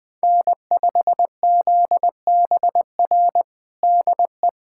Morse code, heard as a keyed tone, about 20 wpm, 700 Hz. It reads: N5ZBR DE
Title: USA Callsign DE Callsign 20wpm